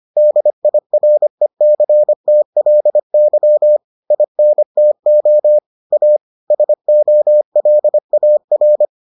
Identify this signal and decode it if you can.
Morse code, heard as a keyed tone, about 25 wpm, 600 Hz. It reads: DIRECTLY INTO A SOLAR